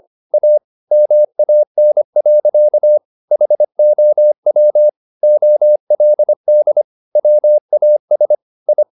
Morse code, heard as a keyed tone, about 25 wpm, 600 Hz. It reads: A MAN. HOW OLD WAS I